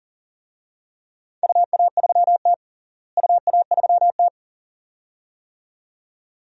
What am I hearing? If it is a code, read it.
Morse code, heard as a keyed tone, about 40 wpm, 700 Hz. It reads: UA3T UA3T
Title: DXCC-100 Call Signs - 4 Characters - Repeated 2x - 40wpm